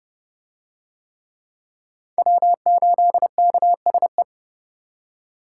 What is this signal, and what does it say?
Morse code, heard as a keyed tone, about 25 wpm, 700 Hz. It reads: W8KSE